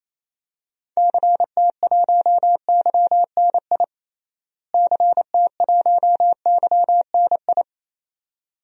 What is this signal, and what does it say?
Morse code, heard as a keyed tone, about 28 wpm, 700 Hz. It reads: CT1YNI CT1YNI
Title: DXCC-100 Call Signs - 6 Characters - Repeated 2x - 28wpm